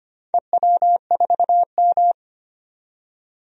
Morse code, heard as a keyed tone, about 25 wpm, 690 Hz. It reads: EW4M
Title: DXCC-100 Call Signs - 4 Characters - 25wpm